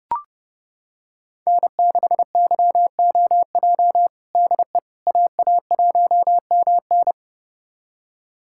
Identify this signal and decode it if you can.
Morse code, heard as a keyed tone, about 30 wpm, 700 Hz. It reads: N6YOJ DE AA1MN
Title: USA Callsign DE Callsign - 5 Characters - 30wpm